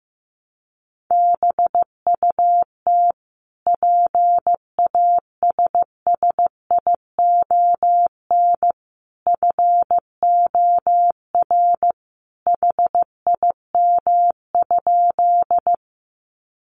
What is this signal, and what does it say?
Morse code, heard as a keyed tone, about 15 wpm, 700 Hz. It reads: BUT PASSION FOR HIM?
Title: Sentences from Top 1300 Words 15wpm